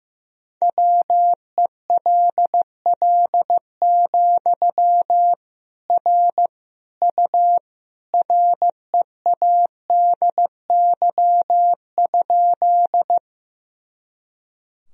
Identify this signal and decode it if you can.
Morse code, heard as a keyed tone, about 16 wpm, 700 Hz. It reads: WELL, R U READY?
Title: Sentences from Top 400 Words and Abbreviations 15wpm